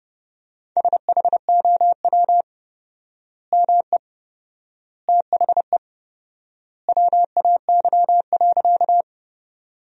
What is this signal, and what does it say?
Morse code, heard as a keyed tone, about 30 wpm, 700 Hz. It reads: SHOW ME THE WAY.
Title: 4x Word-Spacing - Sentences from Top 200 Words - 30wpm